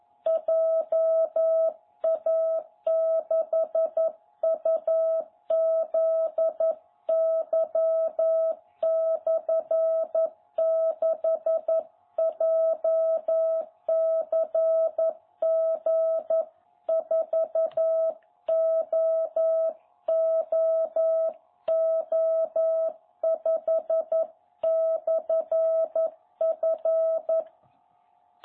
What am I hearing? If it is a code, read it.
Morse code, heard as a keyed tone, about 11 wpm, 660 Hz. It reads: JA6UZY/6 JCG4OOO5/F
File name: jh8cbh / 2021cw